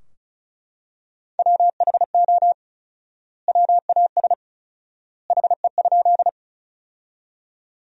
Morse code, heard as a keyed tone, about 36 wpm, 700 Hz. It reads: WHO WAS HE?
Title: 4x Word-Spacing - Sentences from Top 100 Words - 35wpm